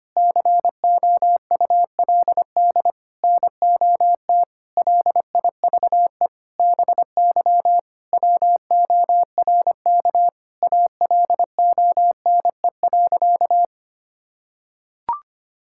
Morse code, heard as a keyed tone, about 25 wpm, 700 Hz. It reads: COULD NOT LIVE BY WORK ALONE.
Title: Sentences from Top 1600 Words 25wpm